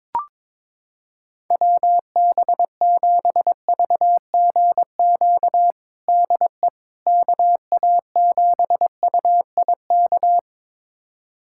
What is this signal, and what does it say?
Morse code, heard as a keyed tone, about 22 wpm, 700 Hz. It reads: WB7VGQ DE KA7UIK
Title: USA Callsign DE Callsign - 6 Characters - 22wpm